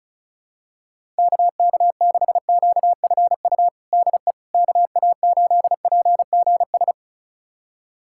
Morse code, heard as a keyed tone, about 36 wpm, 700 Hz. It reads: KK6QFU DE KA8PGS
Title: USA Callsign DE Callsign - 6 Characters - 35wpm